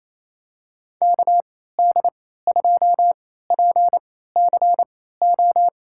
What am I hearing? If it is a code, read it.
Morse code, heard as a keyed tone, about 28 wpm, spaced as for 16 wpm, 700 Hz. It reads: KD2PCO
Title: USA Call Signs - 6 Characters - 3x Character Spacing - 28wpm